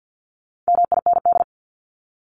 Morse code, heard as a keyed tone, about 50 wpm, 700 Hz. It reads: MIND
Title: Top 500 Words 50wpm